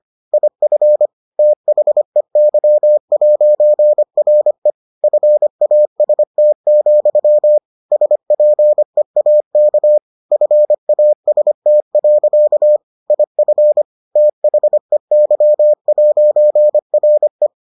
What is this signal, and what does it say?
Morse code, heard as a keyed tone, about 25 wpm, 600 Hz. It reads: IF THEY'RE FAST, SPEAK FAST. IF THEY'RE